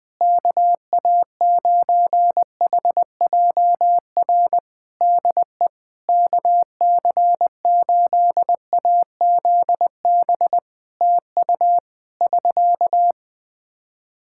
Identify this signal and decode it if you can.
Morse code, heard as a keyed tone, about 20 wpm, 700 Hz. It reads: KA9HJR DE KC8AZB TU <SK>
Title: Practice QSO - Exchange 4 - Ending - 20wpm